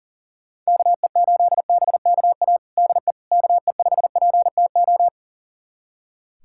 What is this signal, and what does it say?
Morse code, heard as a keyed tone, about 40 wpm, 690 Hz. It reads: KE8BKA DE KE5PTO
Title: USA Callsign DE Callsign - 6 Characters - 40wpm